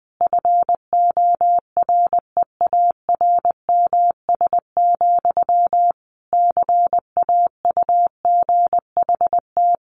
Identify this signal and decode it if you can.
Morse code, heard as a keyed tone, about 19 wpm, 700 Hz. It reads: FOREARMS, CAUGHT
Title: A Sound Of Thunder 20wpm